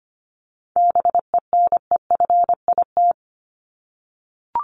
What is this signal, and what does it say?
Morse code, heard as a keyed tone, about 25 wpm, 700 Hz. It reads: BENEFIT
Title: Top 751-1000 Words 25wpm